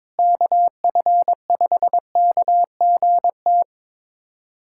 Morse code, heard as a keyed tone, about 22 wpm, 700 Hz. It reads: KF5KGT
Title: USA Callsign DE Callsign - 6 Characters - 22wpm